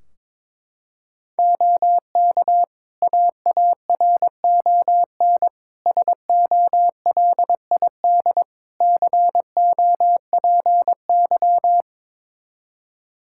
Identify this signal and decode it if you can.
Morse code, heard as a keyed tone, about 22 wpm, 700 Hz. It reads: OK AARON SOLID COPY
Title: K1USN SST QSO Practice 22wpm